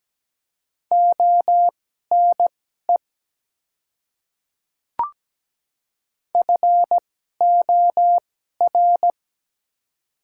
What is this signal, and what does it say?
Morse code, heard as a keyed tone, about 17 wpm, spaced as for 12 wpm, 700 Hz. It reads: ONE FOR
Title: 2x Character Spacing - Top 100 Words - 17wpm